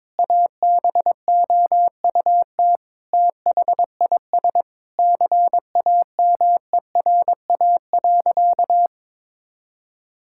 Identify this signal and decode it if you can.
Morse code, heard as a keyed tone, about 22 wpm, 700 Hz. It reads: ABOUT THIS CAMERA.
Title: Sentences from Top 1000 Words - Review Part 5 - 22wpm